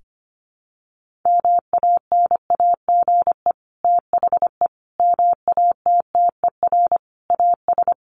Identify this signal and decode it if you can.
Morse code, heard as a keyed tone, about 25 wpm, 700 Hz. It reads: MANAGE THE MATTER AS
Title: Sentences from Top 1400 Words 25wpm